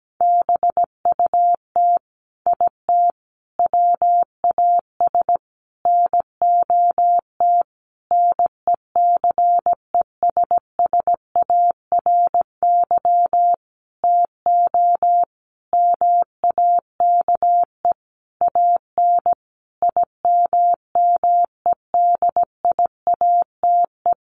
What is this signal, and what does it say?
Morse code, heard as a keyed tone, about 17 wpm, 700 Hz. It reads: BUT IT WAS NOT NECESSARY TO MAKE AN IMMEDIATE
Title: Sentences from Top 1200 Words 17wpm